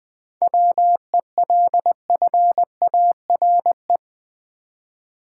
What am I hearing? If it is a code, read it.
Morse code, heard as a keyed tone, about 20 wpm, 700 Hz. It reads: WELFARE
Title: 7 Letter Words - First 3 Letters Given - 20wpm